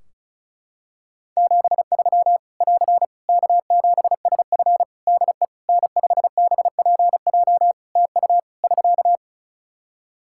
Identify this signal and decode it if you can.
Morse code, heard as a keyed tone, about 34 wpm, 700 Hz. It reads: 73 <AR> K7SF DE N5BPJ TU <SK>
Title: Practice QSO - Exchange 4 - Ending - 35wpm